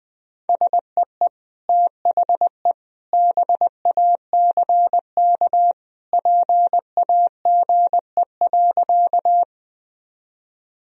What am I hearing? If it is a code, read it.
Morse code, heard as a keyed tone, about 20 wpm, 700 Hz. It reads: SEE THE BACK PAGE.
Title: Sentences from Top 1000 Words - Review Part 1 - 20wpm